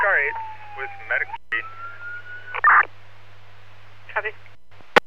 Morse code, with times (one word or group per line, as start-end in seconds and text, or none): none